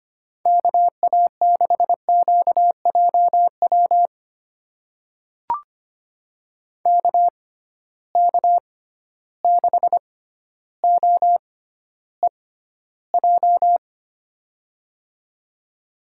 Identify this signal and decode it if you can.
Morse code, heard as a keyed tone, about 25 wpm, spaced as for 13 wpm, 700 Hz. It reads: KA6QJW KK6OEJ
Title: USA Call Signs - 6 Characters - 6x Character Spacing - 25wpm